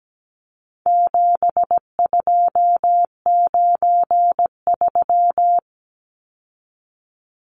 Morse code, heard as a keyed tone, about 17 wpm, 700 Hz. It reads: 7293
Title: Numbers - 4-Digit 17wpm